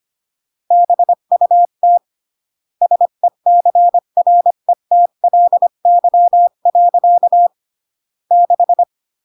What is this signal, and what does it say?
Morse code, heard as a keyed tone, about 25 wpm, 700 Hz. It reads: BUT SECRETLY. 6